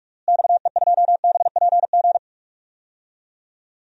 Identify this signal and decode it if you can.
Morse code, heard as a keyed tone, about 46 wpm, 700 Hz. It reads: XE2BPG
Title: DXCC-100 Call Signs - 6 Characters - Repeated 3x - 45wpm